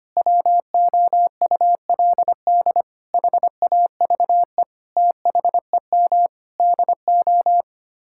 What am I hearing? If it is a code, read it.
Morse code, heard as a keyed tone, about 25 wpm, 700 Hz. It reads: WOULD HAVE THEM DO
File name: Sets of 9 Words - 25wpm